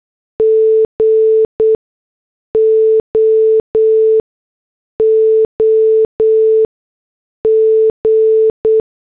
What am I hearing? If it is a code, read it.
Morse code, heard as a keyed tone, about 8 wpm, spaced as for 6 wpm, 440 Hz. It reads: GOOG